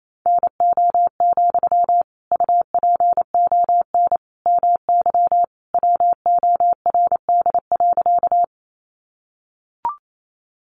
Morse code, heard as a keyed tone, about 28 wpm, 700 Hz. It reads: NO, UPON MY WORD.
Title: Sentences from Top 1100 Words 28wpm